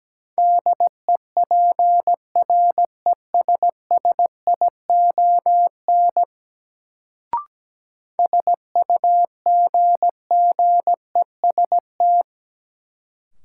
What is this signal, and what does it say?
Morse code, heard as a keyed tone, about 17 wpm, 700 Hz. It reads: DEPRESSION SUGGEST
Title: Sentences from Top 1300 Words - New Word Drill Alt - 17wpm